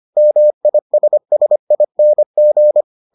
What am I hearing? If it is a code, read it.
Morse code, heard as a keyed tone, about 25 wpm, 600 Hz. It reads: MISSING